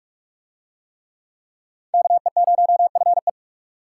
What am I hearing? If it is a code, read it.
Morse code, heard as a keyed tone, about 43 wpm, 700 Hz. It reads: KE0FE